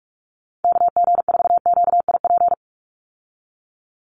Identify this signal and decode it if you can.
Morse code, heard as a keyed tone, about 44 wpm, 700 Hz. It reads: KG4QIP
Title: USA Call Signs 45wpm